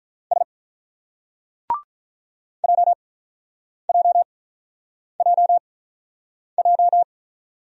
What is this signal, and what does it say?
Morse code, heard as a keyed tone, about 47 wpm, spaced as for 9 wpm, 700 Hz. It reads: S JJJJ